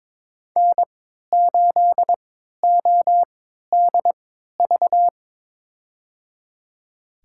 Morse code, heard as a keyed tone, about 22 wpm, spaced as for 13 wpm, 700 Hz. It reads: N8ODV
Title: USA Call Signs - 5 Characters - 3x Character Spacing - 22wpm